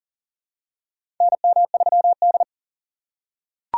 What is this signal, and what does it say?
Morse code, heard as a keyed tone, about 40 wpm, 700 Hz. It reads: NM3D
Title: USA Call Signs - 4 Characters - 2x Character Spacing - 40wpm